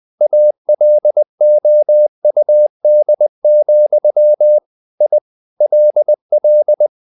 Morse code, heard as a keyed tone, about 20 wpm, 600 Hz. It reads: ALOUD, I LL
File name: the old man and the sea / the old man and the sea part 1 20wpm